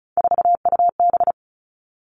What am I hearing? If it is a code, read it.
Morse code, heard as a keyed tone, about 34 wpm, 700 Hz. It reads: 4UB
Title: All Cut Numbers - 3-Digit 35wpm